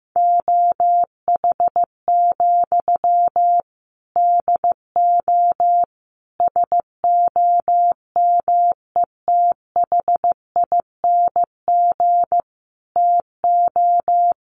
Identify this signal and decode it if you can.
Morse code, heard as a keyed tone, about 15 wpm, 700 Hz. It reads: OH, DO SOMETHING TO
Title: Sentences from Top 1100 Words 15wpm